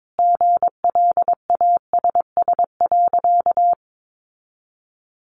Morse code, heard as a keyed tone, about 22 wpm, 700 Hz. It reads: GLASS.
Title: Sentences from Top 1300 Words 22wpm